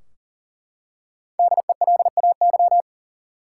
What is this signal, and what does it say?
Morse code, heard as a keyed tone, about 40 wpm, 700 Hz. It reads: DELAY
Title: Five Letter Words - 40wpm